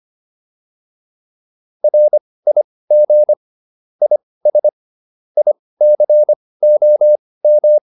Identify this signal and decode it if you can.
Morse code, heard as a keyed tone, about 25 wpm, spaced as for 18 wpm, 600 Hz. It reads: RIG IS ICOM